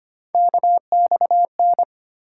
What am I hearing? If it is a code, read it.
Morse code, heard as a keyed tone, about 25 wpm, 700 Hz. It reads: KXN
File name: Three Letter-Number Combinations - 25wpm